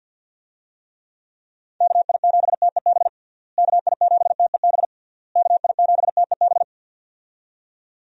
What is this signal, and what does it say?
Morse code, heard as a keyed tone, about 47 wpm, 700 Hz. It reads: KI7TEB KI7TEB KI7TEB
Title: USA Call Signs - 6 Characters - Repeated 3x - 50wpm